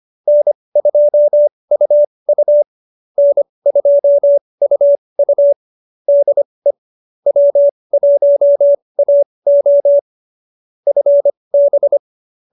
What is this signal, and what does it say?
Morse code, heard as a keyed tone, about 25 wpm, spaced as for 20 wpm, 590 Hz. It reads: N2UU N2UU DE W1AO FB
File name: QSO313 20